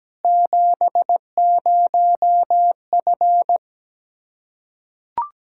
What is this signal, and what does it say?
Morse code, heard as a keyed tone, about 17 wpm, 700 Hz. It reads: 70F
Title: K1USN SST QSO Practice 17wpm